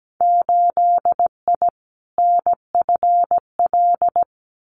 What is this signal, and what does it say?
Morse code, heard as a keyed tone, about 17 wpm, 700 Hz. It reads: 8I NFL
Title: Winter Field Day Exchanges Encore 17wpm